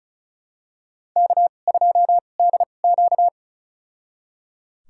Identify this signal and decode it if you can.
Morse code, heard as a keyed tone, about 34 wpm, spaced as for 25 wpm, 700 Hz. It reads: K2DQ
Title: USA Call Signs - 2x Character Spacing - 35wpm